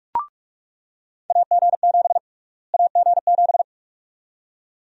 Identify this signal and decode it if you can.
Morse code, heard as a keyed tone, about 46 wpm, 700 Hz. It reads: AG7 AG7
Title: USA Call Sign Prefixes - Repeated 2x - 45wpm